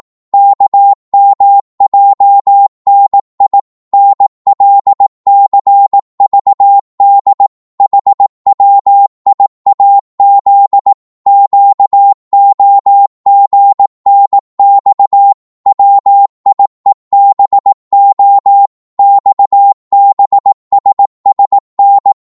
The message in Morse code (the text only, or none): KMJNI NLCVD HWIAZ QOGNX WIEBO XBSSN